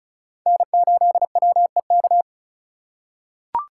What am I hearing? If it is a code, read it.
Morse code, heard as a keyed tone, about 36 wpm, 700 Hz. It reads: N8WEK
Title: USA Call Signs - 5 Characters - 2x Character Spacing - 35wpm